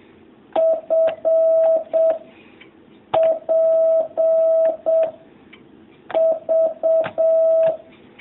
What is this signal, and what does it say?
Morse code, heard as a keyed tone, about 7 wpm, 650 Hz. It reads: F P V
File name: jh8cbh / 2019cw